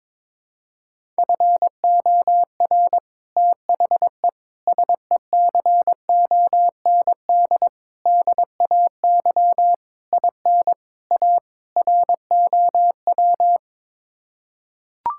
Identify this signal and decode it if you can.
Morse code, heard as a keyed tone, about 22 wpm, 700 Hz. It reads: FOR THE SECOND DAY IN A ROW
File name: practice / 22wpm / Sets of 7 Words - 22wpm